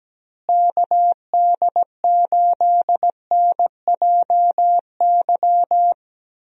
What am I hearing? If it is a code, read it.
Morse code, heard as a keyed tone, about 17 wpm, 700 Hz. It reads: KD8NJY
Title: USA Call Signs - 6 Characters - Repeated 2x - 17wpm